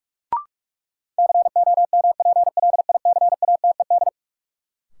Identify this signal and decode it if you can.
Morse code, heard as a keyed tone, about 46 wpm, 700 Hz. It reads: COMPLICATED
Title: Sentences from Top 1200 Words - New Word Drill Alt - 45wpm